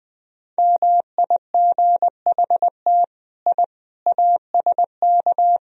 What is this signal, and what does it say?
Morse code, heard as a keyed tone, about 20 wpm, 700 Hz. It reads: MIGHT I ASK
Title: Sentences from Top 1000 Words - Review Part 3 - 20wpm